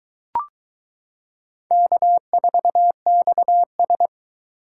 Morse code, heard as a keyed tone, about 22 wpm, 700 Hz. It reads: K4XS